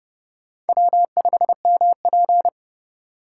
Morse code, heard as a keyed tone, about 32 wpm, 700 Hz. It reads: W5MP
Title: USA Call Signs - 4 Characters - 4x Character Spacing - 30wpm